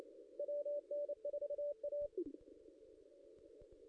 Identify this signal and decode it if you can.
Morse code, heard as a keyed tone, about 28 wpm, 580 Hz. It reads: WN4A
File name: contests / ss-cw-k5tr-op-n5rz -0600